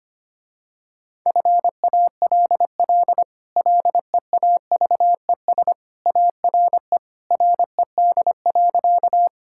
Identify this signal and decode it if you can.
Morse code, heard as a keyed tone, about 25 wpm, 700 Hz. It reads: FALL LEAVES ARE RED.
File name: Easy Sentences with Hints - 25wpm